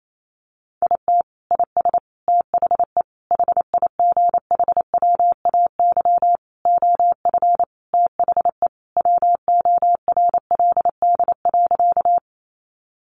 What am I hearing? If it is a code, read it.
Morse code, heard as a keyed tone, about 28 wpm, 700 Hz. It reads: IT IS THE HIGHWAY OF THE WORLD.
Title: Sentences from Top 1400 Words 28wpm